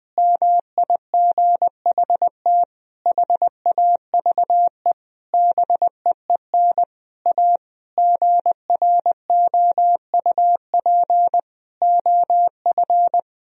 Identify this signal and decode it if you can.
Morse code, heard as a keyed tone, about 20 wpm, 700 Hz. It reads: MIGHT HAVE BEEN A GROUP OF
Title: Sentences from Top 1000 Words - Review Part 1 - 20wpm